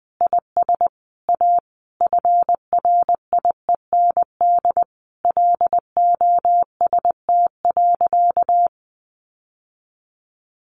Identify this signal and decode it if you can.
Morse code, heard as a keyed tone, about 20 wpm, 700 Hz. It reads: IS A FRIEND LOST.
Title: Sentences from Top 1400 Words 20wpm